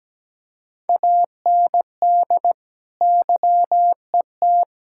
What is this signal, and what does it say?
Morse code, heard as a keyed tone, about 17 wpm, 700 Hz. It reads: AND YET
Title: Sentences from Top 1000 Words - Review Part 6 - 17wpm